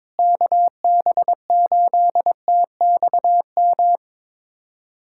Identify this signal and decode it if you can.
Morse code, heard as a keyed tone, about 22 wpm, 700 Hz. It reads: KB8TXM
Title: USA Callsign DE Callsign 22wpm